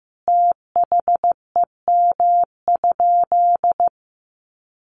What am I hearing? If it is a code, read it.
Morse code, heard as a keyed tone, about 15 wpm, 700 Hz. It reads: THEM?
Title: Sentences from Top 1300 Words 15wpm